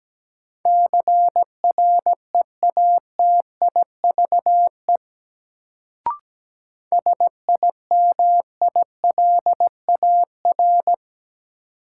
Morse code, heard as a keyed tone, about 17 wpm, 700 Hz. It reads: CREATIVE SIMILAR
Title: Sentences from Top 1000 Words - Review Part 5 - New Word Drill - 17wpm